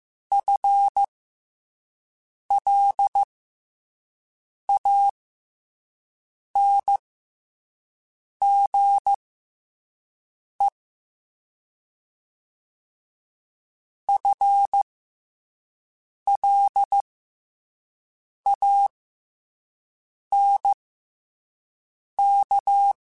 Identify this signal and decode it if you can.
Morse code, heard as a keyed tone, about 15 wpm, spaced as for 5 wpm, 800 Hz. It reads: FLANGE FLANK
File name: FL Runs / FL Runs mp3s / FL Runs @5